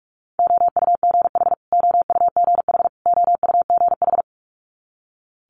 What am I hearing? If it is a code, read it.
Morse code, heard as a keyed tone, about 46 wpm, 690 Hz. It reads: OUGH OUGH OUGH
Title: Multi-letter Phonemes Repeated 3x 45wpm